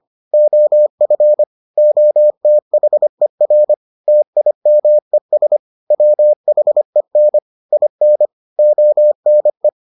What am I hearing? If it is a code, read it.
Morse code, heard as a keyed tone, about 25 wpm, 600 Hz. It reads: OF OTHER TIMES WHEN IN ONE